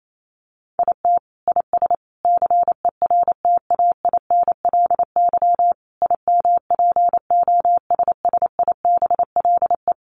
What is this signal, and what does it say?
Morse code, heard as a keyed tone, about 28 wpm, 700 Hz. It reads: IT IS CERTAINLY IMPOSSIBLE
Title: Sentences from Top 1100 Words 28wpm